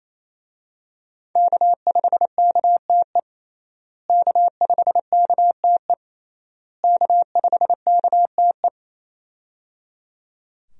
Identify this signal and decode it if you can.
Morse code, heard as a keyed tone, about 28 wpm, 700 Hz. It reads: K5KTE K5KTE K5KTE
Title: USA Call Signs - 5 Characters - Repeated 3x - 28wpm